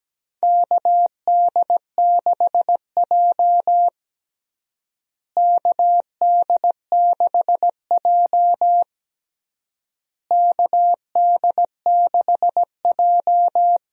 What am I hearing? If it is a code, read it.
Morse code, heard as a keyed tone, about 17 wpm, 700 Hz. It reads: KD6J KD6J KD6J
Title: USA Call Signs - 4 Characters - Repeated 3x - 17wpm